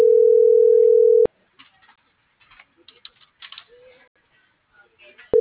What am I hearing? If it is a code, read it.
Ringback tone (North America), 440 + 480 Hz.